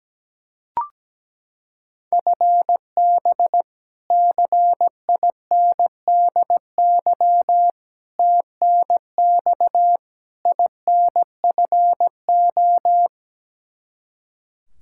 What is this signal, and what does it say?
Morse code, heard as a keyed tone, about 17 wpm, 700 Hz. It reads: FB CINDY TNX INFO
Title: Practice QSO - Exchange 3 - More Info - 17wpm